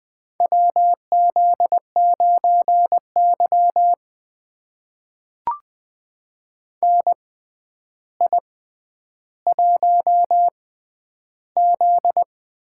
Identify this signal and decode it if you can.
Morse code, heard as a keyed tone, about 21 wpm, spaced as for 10 wpm, 700 Hz. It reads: WZ9Y NI1Z
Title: USA Call Signs - 4 Characters - 6x Character Spacing - 20wpm